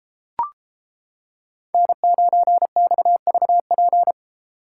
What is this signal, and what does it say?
Morse code, heard as a keyed tone, about 33 wpm, 700 Hz. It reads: N9XVP